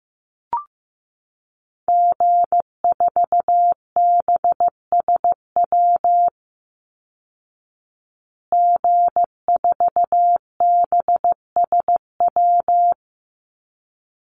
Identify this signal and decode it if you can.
Morse code, heard as a keyed tone, about 15 wpm, 700 Hz. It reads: G4BSW G4BSW
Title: Call Signs Repeated 3x - 15wpm